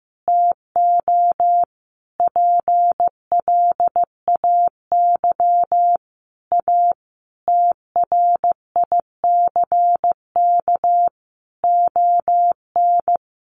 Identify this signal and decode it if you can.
Morse code, heard as a keyed tone, about 15 wpm, 700 Hz. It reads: TO PLAY A TRICK ON
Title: Sentences from Top 1500 Words 15wpm